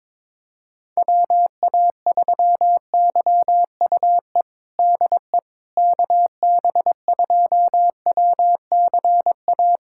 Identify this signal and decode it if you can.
Morse code, heard as a keyed tone, about 22 wpm, 700 Hz. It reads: WA3YUE DE KB2WCA
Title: USA Callsign DE Callsign 22wpm